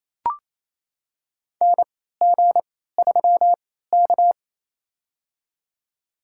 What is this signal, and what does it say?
Morse code, heard as a keyed tone, about 28 wpm, spaced as for 16 wpm, 700 Hz. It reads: NG3K
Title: USA Call Signs - 4 Characters - 3x Character Spacing - 28wpm